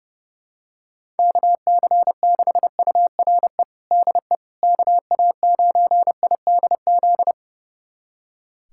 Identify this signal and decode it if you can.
Morse code, heard as a keyed tone, about 30 wpm, 700 Hz. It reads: KC6URE DE KA9IDZ
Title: USA Callsign DE Callsign - 6 Characters - 30wpm